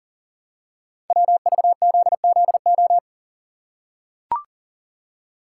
Morse code, heard as a keyed tone, about 38 wpm, 690 Hz. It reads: WVZZO
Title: Code Groups - 5 Letters - 40wpm